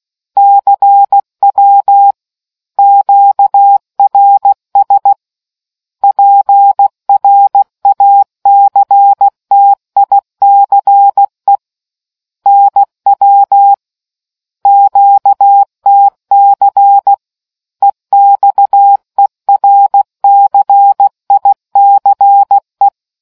Morse code, heard as a keyed tone, about 16 wpm, 800 Hz. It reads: CW QRS PRACTICE NW QTC EXERCICE